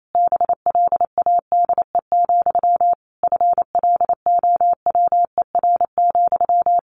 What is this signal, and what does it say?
Morse code, heard as a keyed tone, about 28 wpm, 700 Hz. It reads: BLADE, FLOWER,
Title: A Sound Of Thunder 28wpm